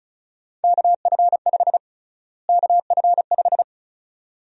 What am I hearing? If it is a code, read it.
Morse code, heard as a keyed tone, about 36 wpm, 700 Hz. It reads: KF5 KF5
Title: USA Call Sign Prefixes - Repeated 2x - 35wpm